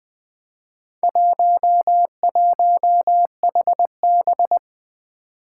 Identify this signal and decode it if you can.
Morse code, heard as a keyed tone, about 20 wpm, 700 Hz. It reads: 11HB
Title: Four Challenging Letter-Number Combinations 20wpm